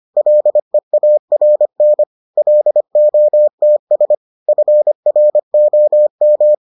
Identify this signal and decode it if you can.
Morse code, heard as a keyed tone, about 25 wpm, 600 Hz. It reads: LEARN LOTS FROM